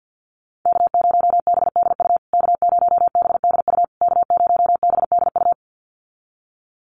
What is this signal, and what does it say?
Morse code, heard as a keyed tone, about 49 wpm, 700 Hz. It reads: K0BDU K0BDU K0BDU
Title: USA Call Signs Repeated 3x 50wpm